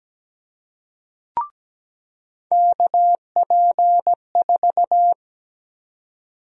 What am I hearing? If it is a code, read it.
Morse code, heard as a keyed tone, about 17 wpm, 700 Hz. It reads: KP4
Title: USA Call Sign Prefixes - 17wpm